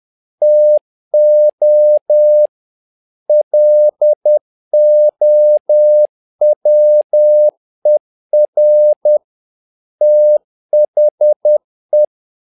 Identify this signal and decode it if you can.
Morse code, heard as a keyed tone, about 10 wpm, 600 Hz. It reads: TO LOWER THE
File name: learning center / tower trouble / tower trouble 10wpm